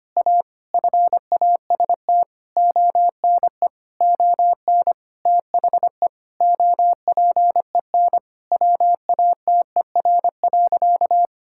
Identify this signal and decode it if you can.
Morse code, heard as a keyed tone, about 25 wpm, 700 Hz. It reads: A FAST ONE ON THE OPEN WATER.
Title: Sentences from Top 1000 Words - Review Part 6 - 25wpm